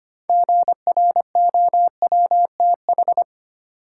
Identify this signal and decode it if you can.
Morse code, heard as a keyed tone, about 25 wpm, 700 Hz. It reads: GROWTH
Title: Sentences from Top 1000 Words - Review Part 5 - New Word Drill - 25wpm